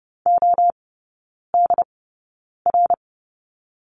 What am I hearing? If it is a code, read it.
Morse code, heard as a keyed tone, about 30 wpm, spaced as for 9 wpm, 700 Hz. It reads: ODR
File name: course / 30:10wpm / 087 Lesson 15e - Number 5 - Three Characters 30wpm